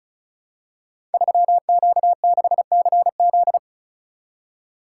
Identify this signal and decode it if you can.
Morse code, heard as a keyed tone, about 36 wpm, 700 Hz. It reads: 3Q6CZ
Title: Code Groups - 5 Letter:Numbers - 35wpm